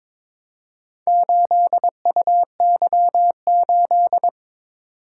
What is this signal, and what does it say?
Morse code, heard as a keyed tone, about 22 wpm, 700 Hz. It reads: 8UY8
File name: practice / 22wpm / Four Letter-Number Combinations - 22wpm